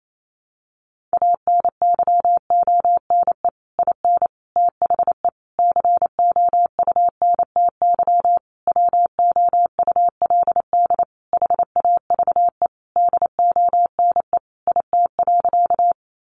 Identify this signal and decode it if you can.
Morse code, heard as a keyed tone, about 28 wpm, 700 Hz. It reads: ANYONE IN THE COUNTY WOULD HAVE DONE IT.
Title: Sentences from Top 1200 Words 28wpm